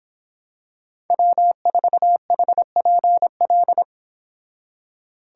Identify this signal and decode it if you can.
Morse code, heard as a keyed tone, about 26 wpm, 700 Hz. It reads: W4HPL